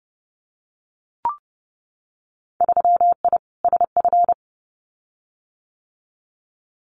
Morse code, heard as a keyed tone, about 30 wpm, 700 Hz. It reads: 3I SF
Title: Winter Field Day Exchanges 30wpm